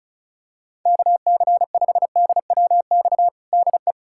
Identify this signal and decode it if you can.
Morse code, heard as a keyed tone, about 36 wpm, 700 Hz. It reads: KC5DWX DE
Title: USA Callsign DE Callsign - 6 Characters - 35wpm